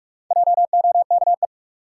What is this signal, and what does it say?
Morse code, heard as a keyed tone, about 42 wpm, 700 Hz. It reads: JOKE
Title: Four Letter Words - 45wpm